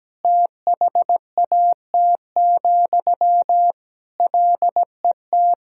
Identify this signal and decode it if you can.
Morse code, heard as a keyed tone, about 17 wpm, 700 Hz. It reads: THAT, LET
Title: Sentences from Top 1000 Words - Review Part 4 - 17wpm